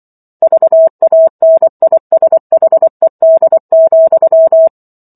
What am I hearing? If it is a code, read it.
Morse code, heard as a keyed tone, about 24 wpm, 650 Hz. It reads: VANISHED,